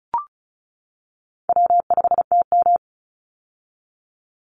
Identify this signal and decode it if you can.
Morse code, heard as a keyed tone, about 36 wpm, 700 Hz. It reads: W5TM
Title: Call Signs - 35wpm